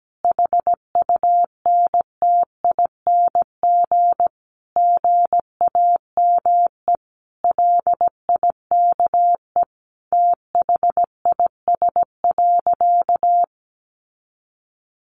Morse code, heard as a keyed tone, about 17 wpm, 700 Hz. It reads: HUNTING GAME LIKE THIS.
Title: A Sound Of Thunder 17wpm